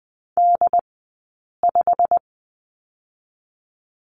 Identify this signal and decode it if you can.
Morse code, heard as a keyed tone, about 20 wpm, 700 Hz. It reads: D 5
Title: 086 Lesson 15d - Number 5 - Two Characters 20wpm